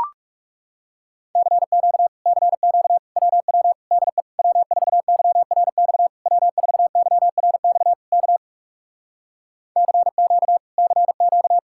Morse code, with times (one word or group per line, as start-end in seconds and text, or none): 1.4-2.1 CQ
2.3-3.0 CQ
3.2-3.7 WW
3.9-4.2 DE
4.4-6.1 W4YRX
6.3-7.9 W4YRX
8.1-8.4 K
9.8-10.6 CQ
10.8-11.6 CQ